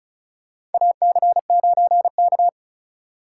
Morse code, heard as a keyed tone, about 34 wpm, 700 Hz. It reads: AC9K
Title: USA Call Signs - 4x Character Spacing - 35wpm